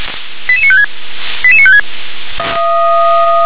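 Five-tone selective call (ZVEI), heard as quick runs of five tones, tones of about 70 ms. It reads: 88956 88956
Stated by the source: ZVEI five-tone selective call received by SDR.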